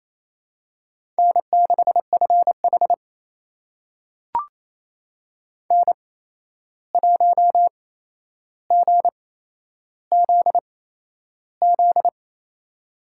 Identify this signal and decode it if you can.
Morse code, heard as a keyed tone, about 28 wpm, spaced as for 11 wpm, 700 Hz. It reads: N6FH N1GZZ
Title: USA Call Signs - 8x Character Spacing - 28wpm